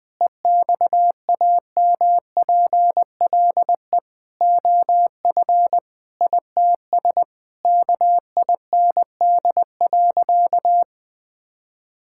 Morse code, heard as a keyed tone, about 21 wpm, 700 Hz. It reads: EXAMPLE OF ITS KIND.
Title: Sentences with 6-Word Phrases - 20wpm